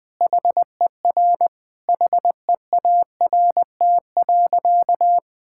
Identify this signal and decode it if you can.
Morse code, heard as a keyed tone, about 20 wpm, 700 Hz. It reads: HER HEART.
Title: Sentences from Top 1600 Words 20wpm